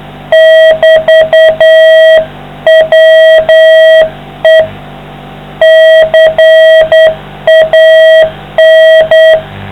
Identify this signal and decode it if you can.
Morse code, heard as a keyed tone, about 9 wpm, 660 Hz. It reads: <BT>WE CAN